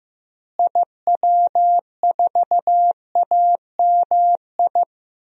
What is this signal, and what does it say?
Morse code, heard as a keyed tone, about 15 wpm, 700 Hz. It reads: IW4AMI
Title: DXCC-100 Call Signs - 6 Characters - Repeated 3x - 15wpm